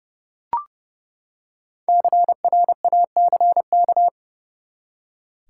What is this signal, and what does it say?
Morse code, heard as a keyed tone, about 30 wpm, 700 Hz. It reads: CRACK
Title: Five Letter Words - 30wpm